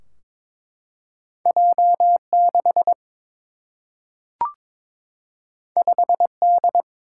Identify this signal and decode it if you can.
Morse code, heard as a keyed tone, about 22 wpm, 700 Hz. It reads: J 6 5 D
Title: Single-Letter Single-Number 22wpm